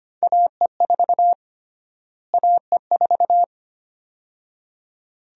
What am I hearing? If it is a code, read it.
Morse code, heard as a keyed tone, about 25 wpm, 700 Hz. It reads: AE4 AE4
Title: USA Call Sign Prefixes - Repeated 2x - 25wpm